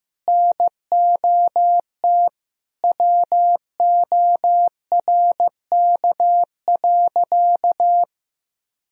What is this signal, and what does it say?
Morse code, heard as a keyed tone, about 16 wpm, 700 Hz. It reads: NOT WORK.
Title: Sentences from Top 1000 Words - Review Part 3 - 15wpm